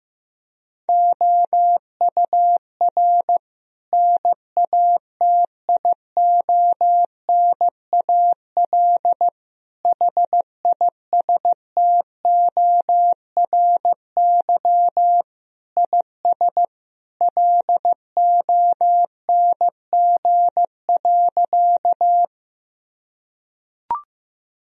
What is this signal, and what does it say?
Morse code, heard as a keyed tone, about 15 wpm, 700 Hz. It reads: OUR NATIONAL HISTORY IS LONG.
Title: Sentences from Top 1000 Words - Review Part 4 - 15wpm